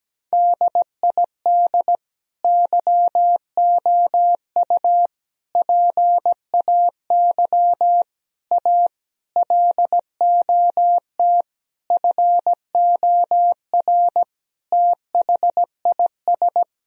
Morse code, heard as a keyed tone, about 17 wpm, 700 Hz. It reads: DID YOU PAY A LOT FOR THIS
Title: Sentences from Top 1000 Words - Review Part 4 - 17wpm